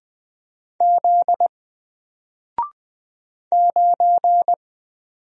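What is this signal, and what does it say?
Morse code, heard as a keyed tone, about 20 wpm, 700 Hz. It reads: Z 9
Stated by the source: Single letter and number Morse drill at 20 wpm, from N4LCD.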